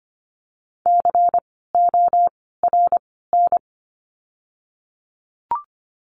Morse code, heard as a keyed tone, about 25 wpm, spaced as for 16 wpm, 700 Hz. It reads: CORN
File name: course / 25:10wpm / 094 Lesson 16f - Letter c - Words 25wpm